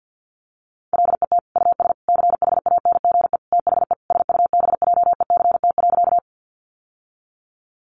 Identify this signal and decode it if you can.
Morse code, heard as a keyed tone, about 50 wpm, 700 Hz. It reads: LET US CHANGE THE SUBJECT.
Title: Sentences from Top 800 Words 50wpm